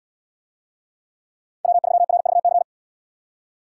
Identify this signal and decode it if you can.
Morse code, heard as a keyed tone, about 69 wpm, 700 Hz. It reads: W3RVC